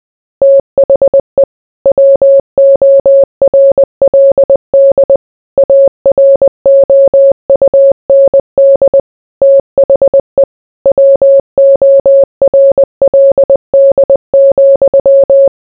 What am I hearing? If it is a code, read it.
Morse code, heard as a keyed tone, about 20 wpm, 550 Hz. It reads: THE WORLD AROUND THE WORLD,